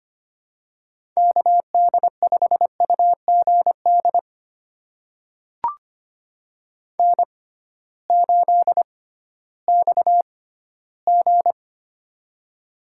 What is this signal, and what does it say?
Morse code, heard as a keyed tone, about 25 wpm, spaced as for 15 wpm, 700 Hz. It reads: KD5UGD N8XG
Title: USA Call Signs - 6x Character Spacing - 25wpm